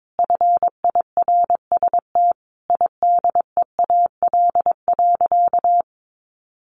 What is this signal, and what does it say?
Morse code, heard as a keyed tone, about 22 wpm, 700 Hz. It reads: FIRST IDEAL.
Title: Sentences from Top 1400 Words 22wpm